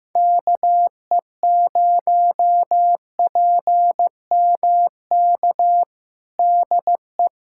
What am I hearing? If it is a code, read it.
Morse code, heard as a keyed tone, about 15 wpm, 700 Hz. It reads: KE0PMK DE
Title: USA Callsign DE Callsign - 6 Characters - 15wpm